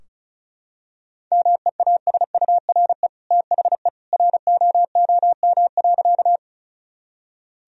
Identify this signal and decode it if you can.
Morse code, heard as a keyed tone, about 36 wpm, 700 Hz. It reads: MEASURE THE ROOM.
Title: Sentences from Top 1000 Words - Review Part 2 - 35wpm